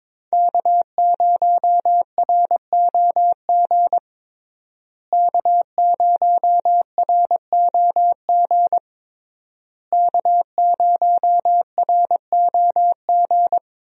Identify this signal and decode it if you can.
Morse code, heard as a keyed tone, about 22 wpm, 700 Hz. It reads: K0ROG K0ROG K0ROG
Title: USA Call Signs - 5 Characters - Repeated 3x - 22wpm